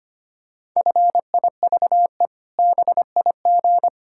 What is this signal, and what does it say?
Morse code, heard as a keyed tone, about 25 wpm, 700 Hz. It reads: FIVE BIG
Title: Pangram Challenge v2 - 25wpm